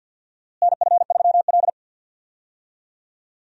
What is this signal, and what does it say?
Morse code, heard as a keyed tone, about 48 wpm, 700 Hz. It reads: NF3L
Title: USA Call Signs - 4 Characters - Repeated 2x - 50wpm